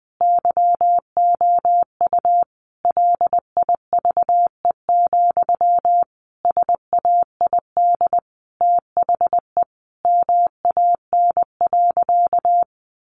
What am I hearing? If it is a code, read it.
Morse code, heard as a keyed tone, about 20 wpm, 700 Hz. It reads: YOU LIVE, SAID THE MAN.
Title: Sentences from Top 200 Words 20wpm